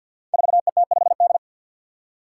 Morse code, heard as a keyed tone, about 45 wpm, 700 Hz. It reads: 4ET5D
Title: Five Challenging Letter-Number Combinations 50wpm